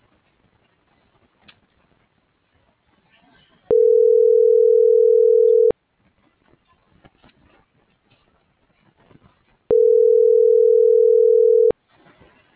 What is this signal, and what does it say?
Ringback tone (North America), 440 + 480 Hz.